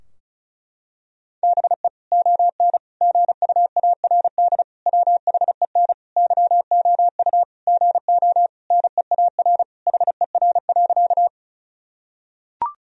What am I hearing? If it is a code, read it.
Morse code, heard as a keyed tone, about 34 wpm, 700 Hz. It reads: BE ON GUARD WHEN YOU GO NEAR HER.
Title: Sentences from Top 1000 Words - Review Part 4 - 35wpm